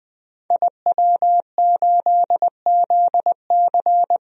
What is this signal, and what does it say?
Morse code, heard as a keyed tone, about 20 wpm, 700 Hz. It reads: IW8ZC
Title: DXCC-100 Call Signs - 5 Characters - Repeated 3x - 20wpm